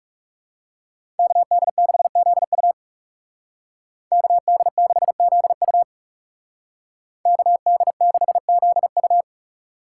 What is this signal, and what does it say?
Morse code, heard as a keyed tone, about 34 wpm, 700 Hz. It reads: KD6ZU KD6ZU KD6ZU
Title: Speed Racing - USA 2x2 Call Signs 30wpm